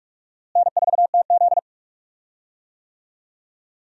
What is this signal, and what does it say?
Morse code, heard as a keyed tone, about 47 wpm, 700 Hz. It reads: N4TZ
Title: USA Call Signs - 4 Characters - Repeated 2x - 45wpm